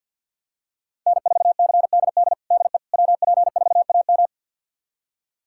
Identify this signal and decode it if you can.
Morse code, heard as a keyed tone, about 51 wpm, 700 Hz. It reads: N4CDD DE WP4AM
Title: USA Callsign DE Callsign - 5 Characters - 50wpm